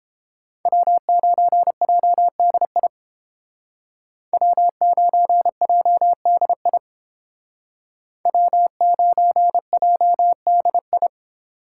Morse code, heard as a keyed tone, about 29 wpm, 700 Hz. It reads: W9JDI W9JDI W9JDI